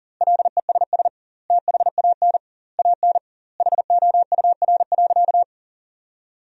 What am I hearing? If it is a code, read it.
Morse code, heard as a keyed tone, about 40 wpm, 700 Hz. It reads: LESS THAN AN HOUR.
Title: 2x Word-Spacing - Sentences from Top 500 Words - 40wpm